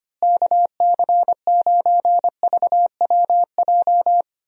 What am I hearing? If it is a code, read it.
Morse code, heard as a keyed tone, about 25 wpm, 700 Hz. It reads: KC9VWJ
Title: USA Call Signs - 6 Characters - Repeated 3x - 25wpm